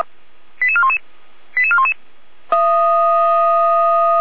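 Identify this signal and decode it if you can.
Five-tone selective call (ZVEI), heard as quick runs of five tones, tones of about 70 ms. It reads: 88411 88411
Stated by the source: ZVEI five-tone selective call received on SDR.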